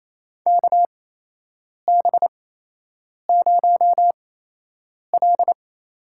Morse code, heard as a keyed tone, about 27 wpm, spaced as for 8 wpm, 700 Hz. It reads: KB0L
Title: USA Call Signs - 4 Characters - 8x Character Spacing - 28wpm